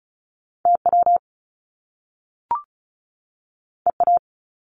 Morse code, heard as a keyed tone, about 36 wpm, 700 Hz. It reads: TW EA
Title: Top Two Letter Combinations in Words 35wpm